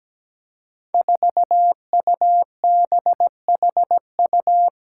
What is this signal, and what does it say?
Morse code, heard as a keyed tone, about 17 wpm, 700 Hz. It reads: 4UBHU
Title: Five Challenging Letter-Number Combinations 17wpm